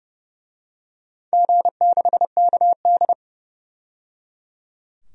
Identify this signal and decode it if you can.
Morse code, heard as a keyed tone, about 30 wpm, 700 Hz. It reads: G6KD
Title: DXCC-100 Call Signs - 4 Characters - 30wpm